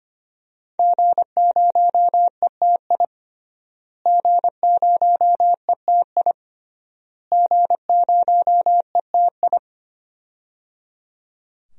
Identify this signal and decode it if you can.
Morse code, heard as a keyed tone, about 25 wpm, 700 Hz. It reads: G0ETI G0ETI G0ETI
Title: DXCC-100 Call Signs - 5 Characters - Repeated 3x - 25wpm